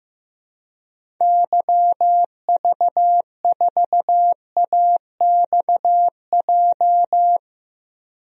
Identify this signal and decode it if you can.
Morse code, heard as a keyed tone, about 15 wpm, 700 Hz. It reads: YV4AXJ
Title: DXCC-100 Call Signs - 6 Characters - 15wpm